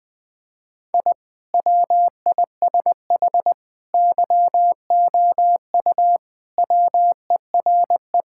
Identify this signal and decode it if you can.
Morse code, heard as a keyed tone, about 20 wpm, 700 Hz. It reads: I WISH YOU WERE
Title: Sentences with 2-Word Phrases - 20wpm